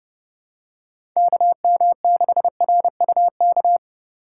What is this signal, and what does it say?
Morse code, heard as a keyed tone, about 28 wpm, 700 Hz. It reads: KM6RUK